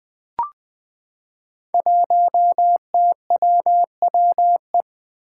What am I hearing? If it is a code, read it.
Morse code, heard as a keyed tone, about 20 wpm, 700 Hz. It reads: 1TWWE
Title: Five Challenging Letter-Number Combinations 20wpm